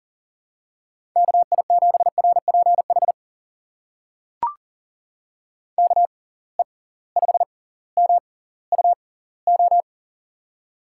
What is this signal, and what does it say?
Morse code, heard as a keyed tone, about 40 wpm, spaced as for 20 wpm, 700 Hz. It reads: KI7RPH KE5MUO
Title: USA Call Signs - 6x Character Spacing - 40wpm